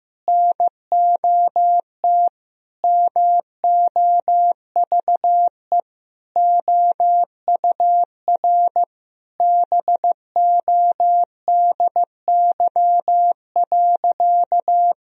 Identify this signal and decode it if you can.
Morse code, heard as a keyed tone, about 15 wpm, 700 Hz. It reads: NOT MOVE OUR BODY.
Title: Sentences from Top 1000 Words - Review Part 2 - 15wpm